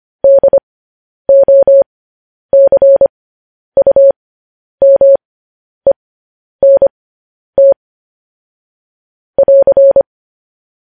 Morse code, heard as a keyed tone, about 25 wpm, spaced as for 10 wpm, 550 Hz. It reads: DOCUMENT <AR>
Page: news headlines in morse code at 10 wpm